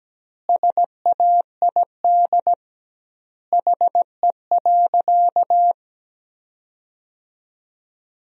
Morse code, heard as a keyed tone, about 17 wpm, 700 Hz. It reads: SAID HE.
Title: 2x Word-Spacing - Sentences from Top 400 Words - 17wpm